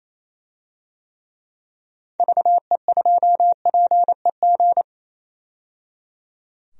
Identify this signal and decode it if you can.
Morse code, heard as a keyed tone, about 28 wpm, 700 Hz. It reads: VE2PEG
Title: DXCC-100 Call Signs - 6 Characters - 28wpm